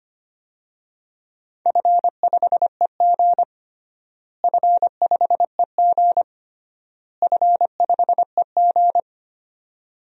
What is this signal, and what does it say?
Morse code, heard as a keyed tone, about 25 wpm, 700 Hz. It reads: F5EG F5EG F5EG
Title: DXCC-100 Call Signs - 4 Characters - Repeated 3x - 25wpm